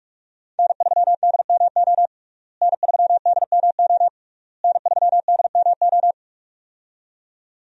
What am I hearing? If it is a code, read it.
Morse code, heard as a keyed tone, about 45 wpm, 700 Hz. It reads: N3DMO N3DMO N3DMO
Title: USA Call Signs - 5 Characters - Repeated 3x - 45wpm